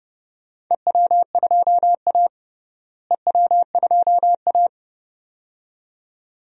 Morse code, heard as a keyed tone, about 30 wpm, 700 Hz. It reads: EW2A EW2A
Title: DXCC-100 Call Signs - 4 Characters - Repeated 2x - 30wpm